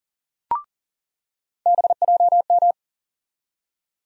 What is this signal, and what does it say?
Morse code, heard as a keyed tone, about 40 wpm, 700 Hz. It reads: BJM
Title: USA Call Sign Postfixes - 3 Letters 40wpm